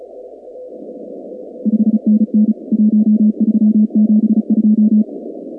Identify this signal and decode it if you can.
Morse code, heard as a keyed tone, about 33 wpm, 210 Hz. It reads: 5NN 1372
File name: contests / 2019-cw-wpx-kt5j-k5pi- -2330 0-16